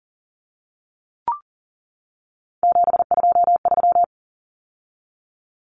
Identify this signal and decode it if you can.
Morse code, heard as a keyed tone, about 40 wpm, 700 Hz. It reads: 723
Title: Common Cut Numbers - 3-Digit 40wpm